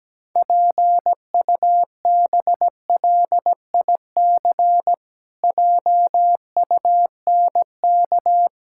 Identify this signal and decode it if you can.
Morse code, heard as a keyed tone, about 17 wpm, 700 Hz. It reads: PUBLIC JUNK
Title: Pangram Challenge v2 - 17wpm